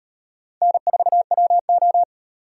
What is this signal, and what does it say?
Morse code, heard as a keyed tone, about 38 wpm, 700 Hz. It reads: N4WO